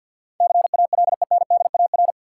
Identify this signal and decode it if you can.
Morse code, heard as a keyed tone, about 52 wpm, 700 Hz. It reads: CALENDAR
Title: Sentences from Top 1600 Words - New Word Drill - 50wpm